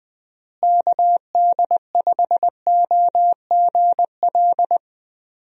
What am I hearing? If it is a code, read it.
Morse code, heard as a keyed tone, about 19 wpm, 700 Hz. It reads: KD5OGL
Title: USA Call Signs - 6 Characters - Repeated 3x - 20wpm